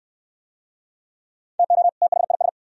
Morse code, heard as a keyed tone, about 62 wpm, 690 Hz. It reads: TO THIS